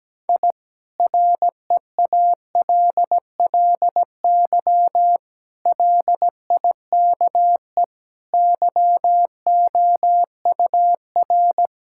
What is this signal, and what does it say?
Morse code, heard as a keyed tone, about 17 wpm, 700 Hz. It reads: I REALLY LIKE YOUR
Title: Sentences from Top 1000 Words - Review Part 6 - 17wpm